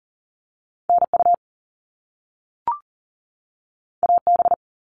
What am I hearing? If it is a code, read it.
Morse code, heard as a keyed tone, about 40 wpm, 700 Hz. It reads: NU AB
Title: Canadian Provinces and Territories 40wpm